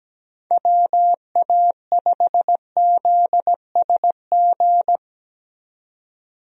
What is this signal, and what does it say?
Morse code, heard as a keyed tone, about 17 wpm, 700 Hz. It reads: WA5ZSG
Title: USA Call Signs - 6 Characters - 17wpm